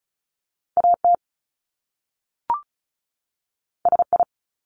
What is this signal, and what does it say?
Morse code, heard as a keyed tone, about 36 wpm, 700 Hz. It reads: AT SI
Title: Top Two Letter Combinations in Words 35wpm